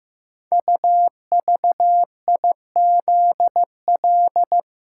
Morse code, heard as a keyed tone, about 15 wpm, 700 Hz. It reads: UVIZL
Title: Code Groups - 5 Letters - 15wpm